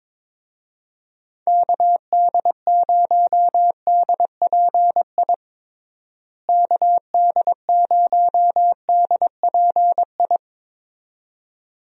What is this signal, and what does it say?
Morse code, heard as a keyed tone, about 22 wpm, 700 Hz. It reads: KD0DPI KD0DPI
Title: USA Call Signs - 6 Characters - Repeated 2x - 22wpm